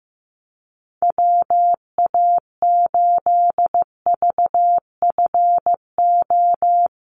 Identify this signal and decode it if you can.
Morse code, heard as a keyed tone, about 15 wpm, 700 Hz. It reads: WA8VFO
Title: USA Call Signs Repeated 3x 15wpm